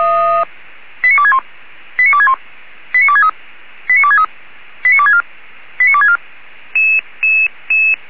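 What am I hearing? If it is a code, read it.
Five-tone selective call (ZVEI), heard as quick runs of five tones, tones of about 70 ms. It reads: 89261 89261 89263 89263 89264 89264
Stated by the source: ZVEI five-tone call received by SDR.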